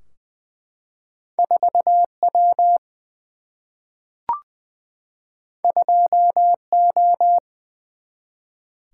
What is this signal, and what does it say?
Morse code, heard as a keyed tone, about 20 wpm, 700 Hz. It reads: 4 W 2 O
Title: Single-Letter Single-Number 20wpm